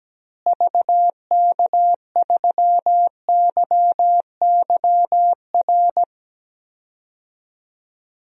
Morse code, heard as a keyed tone, about 17 wpm, 700 Hz. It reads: VK3YYR
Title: DXCC-100 Call Signs - 6 Characters - Repeated 2x - 17wpm